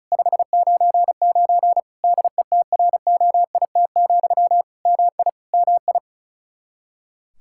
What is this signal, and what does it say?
Morse code, heard as a keyed tone, about 36 wpm, 700 Hz. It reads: 599 DETROIT, MI MI
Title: Practice QSO - Exchange 1 - Beginning - 35wpm